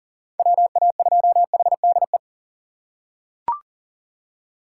Morse code, heard as a keyed tone, about 37 wpm, 700 Hz. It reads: WA2HDE
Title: USA Call Signs - 6 Characters - 2x Character Spacing - 40wpm